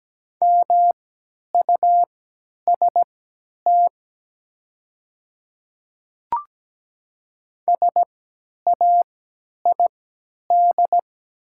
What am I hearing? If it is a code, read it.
Morse code, heard as a keyed tone, about 17 wpm, spaced as for 10 wpm, 700 Hz. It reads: MUST SAID
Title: 3x Character Spacing - Top 300 Words - 17wpm